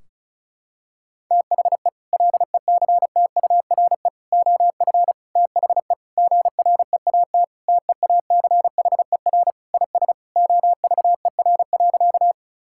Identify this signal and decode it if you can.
Morse code, heard as a keyed tone, about 34 wpm, 700 Hz. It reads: THE LECTURE OF THE GREAT TEACHER IS OVER.
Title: Sentences from Top 1600 Words 35wpm